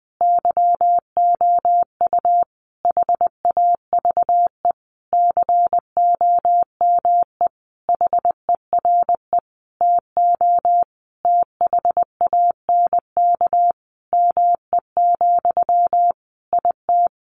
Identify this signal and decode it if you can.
Morse code, heard as a keyed tone, about 20 wpm, 700 Hz. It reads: YOU HAVE COME HERE TO THANK ME, IT
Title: Sentences from Top 800 Words 20wpm